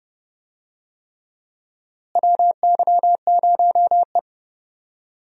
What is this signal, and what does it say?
Morse code, heard as a keyed tone, about 30 wpm, 700 Hz. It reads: WY0E
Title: USA Call Signs - 4 Characters - Repeated 3x - 30wpm